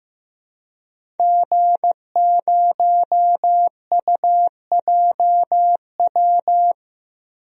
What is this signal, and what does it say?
Morse code, heard as a keyed tone, about 15 wpm, 700 Hz. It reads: G0UJW
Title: DXCC-100 Call Signs - 5 Characters - 15wpm